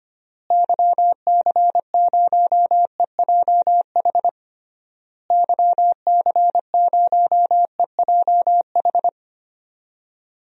Morse code, heard as a keyed tone, about 25 wpm, 700 Hz. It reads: YC0EJH YC0EJH
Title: DXCC-100 Call Signs - 6 Characters - Repeated 3x - 25wpm